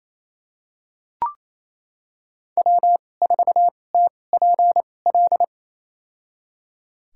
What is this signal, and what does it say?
Morse code, heard as a keyed tone, about 28 wpm, spaced as for 20 wpm, 700 Hz. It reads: W4TPL
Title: USA Call Signs - 5 Characters - 2x Character Spacing - 28wpm